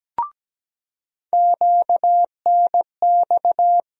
Morse code, heard as a keyed tone, about 17 wpm, 700 Hz. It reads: QNX
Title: USA Call Sign Postfixes - 3 Letters 17wpm